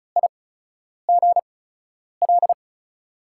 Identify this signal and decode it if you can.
Morse code, heard as a keyed tone, about 36 wpm, spaced as for 10 wpm, 700 Hz. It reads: IGL
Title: USA Call Signs - 8x Character Spacing - 35wpm